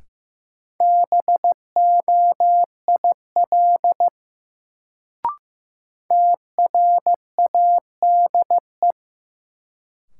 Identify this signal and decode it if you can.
Morse code, heard as a keyed tone, about 15 wpm, 700 Hz. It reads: BOIL TRADE
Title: Sets of 5 Words Encore - New Word Drill - 15wpm